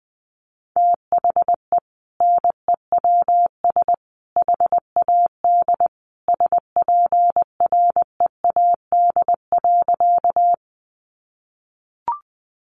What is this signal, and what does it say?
Morse code, heard as a keyed tone, about 19 wpm, 700 Hz. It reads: THE NEWS HAD SPREAD.
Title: Sentences from Top 1200 Words 20wpm